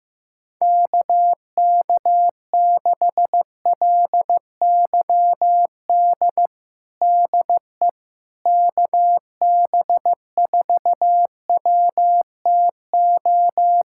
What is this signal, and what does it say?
Morse code, heard as a keyed tone, about 15 wpm, 700 Hz. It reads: KK6LYD DE KB4WTO
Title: USA Callsign DE Callsign - 6 Characters - 15wpm